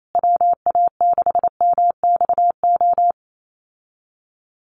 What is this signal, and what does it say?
Morse code, heard as a keyed tone, about 28 wpm, 700 Hz. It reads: WA6MXO
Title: USA Call Signs Repeated 2x 28wpm